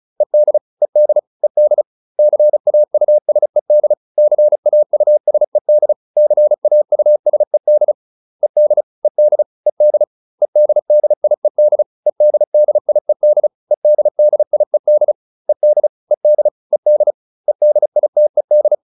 Morse code, heard as a keyed tone, about 34 wpm, 600 Hz. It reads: ED ED ED CAUSED CAUSED CAUSED ED ED ED EDDIED EDDIED EDDIED ED ED ED EDITED